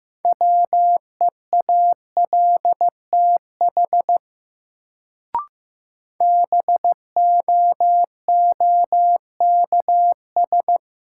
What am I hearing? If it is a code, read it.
Morse code, heard as a keyed tone, about 15 wpm, 700 Hz. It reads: WEALTH BOOKS
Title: Sentences from Top 1000 Words - Review Part 4 - New Word Drill - 15wpm